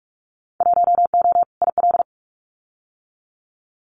Morse code, heard as a keyed tone, about 44 wpm, 700 Hz. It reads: 1O IL
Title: Winter Field Day Exchanges 45wpm